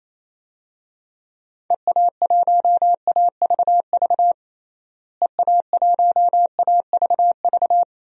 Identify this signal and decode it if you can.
Morse code, heard as a keyed tone, about 27 wpm, 700 Hz. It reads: EA1AVV EA1AVV
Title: DXCC-100 Call Signs - 6 Characters - Repeated 3x - 28wpm